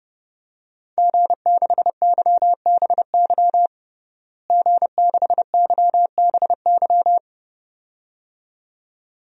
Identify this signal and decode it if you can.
Morse code, heard as a keyed tone, about 30 wpm, 700 Hz. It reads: G6YBY G6YBY
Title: DXCC-100 Call Signs - 5 Characters - Repeated 2x - 30wpm